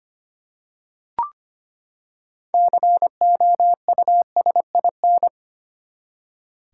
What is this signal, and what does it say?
Morse code, heard as a keyed tone, about 25 wpm, 700 Hz. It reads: COUSIN
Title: Six Letter Words - 25wpm